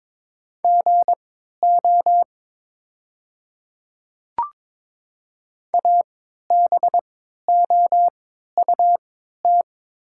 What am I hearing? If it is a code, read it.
Morse code, heard as a keyed tone, about 22 wpm, spaced as for 13 wpm, 690 Hz. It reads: GO ABOUT
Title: 3x Character Spacing - Top 100 Words - 22wpm